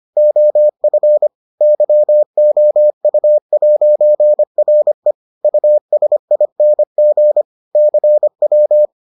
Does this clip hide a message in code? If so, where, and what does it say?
Yes: Morse code from 0.2 to 8.9 s, heard as a keyed tone, about 25 wpm, 600 Hz. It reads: OF YOU'RE USING CW